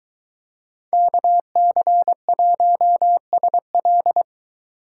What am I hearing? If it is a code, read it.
Morse code, heard as a keyed tone, about 22 wpm, 700 Hz. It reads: KC1SL